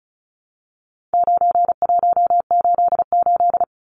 Morse code, heard as a keyed tone, about 34 wpm, 700 Hz. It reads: 9188
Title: Numbers - 4-Digit 35wpm